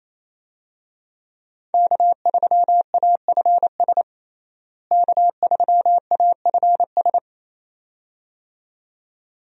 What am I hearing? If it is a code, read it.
Morse code, heard as a keyed tone, about 28 wpm, 700 Hz. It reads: K3AFS K3AFS
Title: USA Call Signs - 5 Characters - Repeated 2x - 28wpm